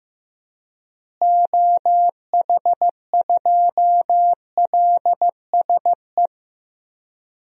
Morse code, heard as a keyed tone, about 15 wpm, 700 Hz. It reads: OH2LSE
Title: DXCC-100 Call Signs - 6 Characters - Repeated 3x - 15wpm